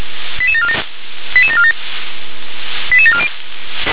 Five-tone selective call (ZVEI), heard as quick runs of five tones, tones of about 70 ms. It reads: 88747 88747 88740